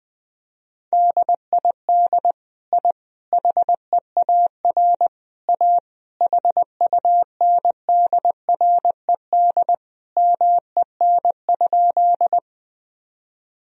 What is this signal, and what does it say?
Morse code, heard as a keyed tone, about 21 wpm, 700 Hz. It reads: DID I HEAR A HUNDRED MEN?
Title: Sentences from Top 1000 Words - Review Part 2 - 20wpm